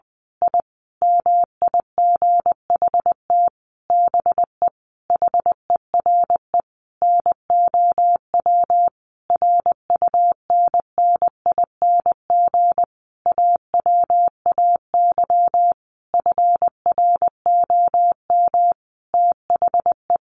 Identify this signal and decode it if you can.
Morse code, heard as a keyed tone, about 20 wpm, 700 Hz. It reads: I MIGHT BE HERE NOW RUNNING AWAY FROM THE
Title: A Sound Of Thunder 20wpm